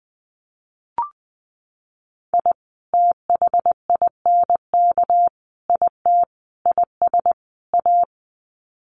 Morse code, heard as a keyed tone, about 20 wpm, 700 Hz. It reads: I THINK IT IS A
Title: Sets of 5 Words Encore 20wpm